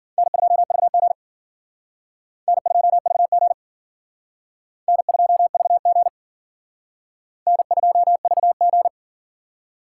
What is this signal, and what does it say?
Morse code, heard as a keyed tone, about 50 wpm, 700 Hz. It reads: N2VG N2VG N2VG N2VG
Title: Speed Racing - USA 1x2 Call Signs 40wpm